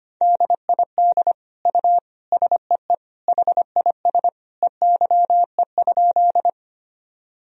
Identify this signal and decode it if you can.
Morse code, heard as a keyed tone, about 25 wpm, 700 Hz. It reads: DID U SEE HIS EYE?
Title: Sentences from Top 300 Words and Abbreviations 25wpm